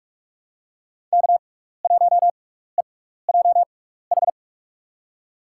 Morse code, heard as a keyed tone, about 44 wpm, spaced as for 15 wpm, 700 Hz. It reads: K1EJH
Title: USA Call Signs - 5 Characters - 6x Character Spacing - 45wpm